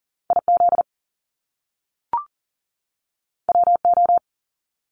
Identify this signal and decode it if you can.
Morse code, heard as a keyed tone, about 40 wpm, 700 Hz. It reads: IZ WO
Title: Top Two Letter Combinations in Words 40wpm